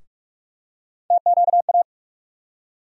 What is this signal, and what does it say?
Morse code, heard as a keyed tone, about 47 wpm, 700 Hz. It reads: TQA
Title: USA Call Sign Postfixes - 3 Letters 45wpm